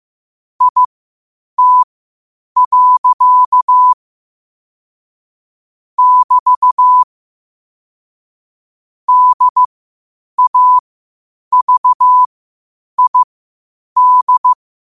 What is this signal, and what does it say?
Morse code, heard as a keyed tone, about 15 wpm, spaced as for 8 wpm, 1000 Hz. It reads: IT. <BT> DAVID
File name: qotdfiles / 8wpm